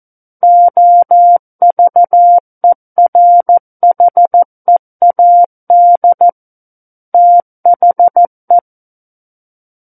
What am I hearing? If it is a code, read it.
Morse code, heard as a keyed tone, about 14 wpm, 700 Hz. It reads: OVERHEAD THE
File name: Book One / 05-Chapter 5 - 14 WPM